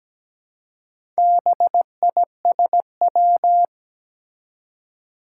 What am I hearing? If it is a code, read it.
Morse code, heard as a keyed tone, about 17 wpm, 700 Hz. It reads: BISW
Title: Four Challenging Letter-Number Combinations 17wpm